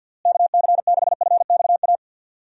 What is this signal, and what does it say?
Morse code, heard as a keyed tone, about 53 wpm, 700 Hz. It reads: KC6FXA
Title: USA Call Signs - 6 Characters - Repeated 2x - 50wpm